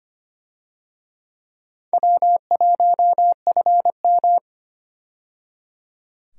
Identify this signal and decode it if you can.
Morse code, heard as a keyed tone, about 25 wpm, 700 Hz. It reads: W1FM
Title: USA Call Signs - 4 Characters - 25wpm